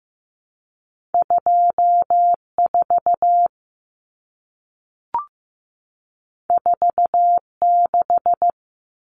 Morse code, heard as a keyed tone, about 15 wpm, 700 Hz. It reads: 24 46
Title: Numbers - 2-Digit 15wpm